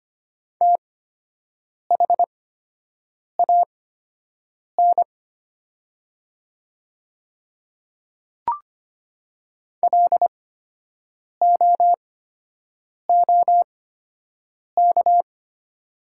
Morse code, heard as a keyed tone, about 25 wpm, spaced as for 7 wpm, 690 Hz. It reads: THAN LOOK
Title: 8x Character Spacing - Top 100 Words - 25wpm